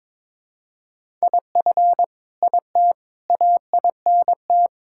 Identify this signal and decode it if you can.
Morse code, heard as a keyed tone, about 21 wpm, 700 Hz. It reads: IF IT AINT
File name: practice / 22wpm / Sets of 7 Words - 22wpm